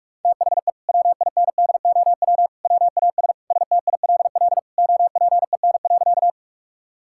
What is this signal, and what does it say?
Morse code, heard as a keyed tone, about 46 wpm, 700 Hz. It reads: THE WINDOW WAS STILL OPEN.
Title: Sentences from Top 1600 Words 45wpm